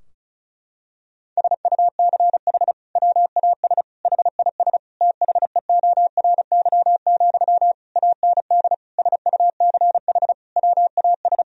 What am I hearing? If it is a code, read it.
Morse code, heard as a keyed tone, about 36 wpm, 700 Hz. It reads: SUCH WAS HIS THEORY, AND SUCH WAS
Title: Sentences from Top 1000 Words - Review Part 4 - 35wpm